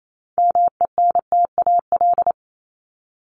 Morse code, heard as a keyed tone, about 28 wpm, 700 Hz. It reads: MENTAL
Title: Top 501-750 Words 28wpm